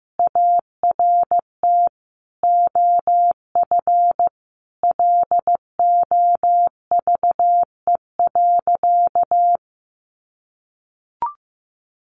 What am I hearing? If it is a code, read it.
Morse code, heard as a keyed tone, about 15 wpm, 700 Hz. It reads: ART OF LOVE.
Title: Sentences from Top 1500 Words 15wpm